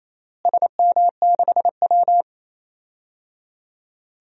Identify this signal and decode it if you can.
Morse code, heard as a keyed tone, about 28 wpm, 700 Hz. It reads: SM6W
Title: DXCC-100 Call Signs - 4 Characters - Repeated 3x - 28wpm